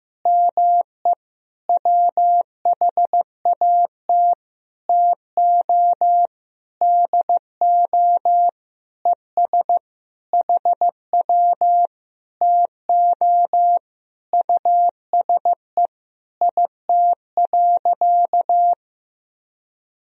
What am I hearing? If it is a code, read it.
Morse code, heard as a keyed tone, about 15 wpm, 700 Hz. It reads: ME WHAT TO DO ES HW TO USE IT.
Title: Sentences from Top 500 Words and Abbreviations 15wpm